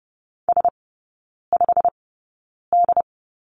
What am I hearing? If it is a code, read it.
Morse code, heard as a keyed tone, about 30 wpm, spaced as for 9 wpm, 700 Hz. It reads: S5D
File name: course / 30:10wpm / 087 Lesson 15e - Number 5 - Three Characters 30wpm